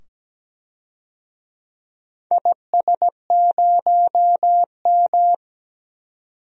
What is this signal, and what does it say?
Morse code, heard as a keyed tone, about 17 wpm, 700 Hz. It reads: IS0M
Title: DXCC-100 Call Signs - 4 Characters - 17wpm